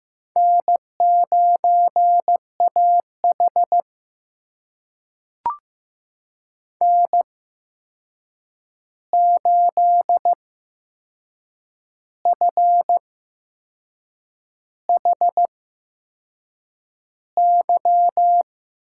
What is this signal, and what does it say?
Morse code, heard as a keyed tone, about 15 wpm, spaced as for 6 wpm, 700 Hz. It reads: N9AH N8FHY
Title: USA Call Signs - 8x Character Spacing - 15wpm